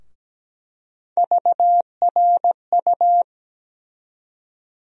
Morse code, heard as a keyed tone, about 17 wpm, 700 Hz. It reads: VRU
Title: USA Call Sign Postfixes - 3 Letters 17wpm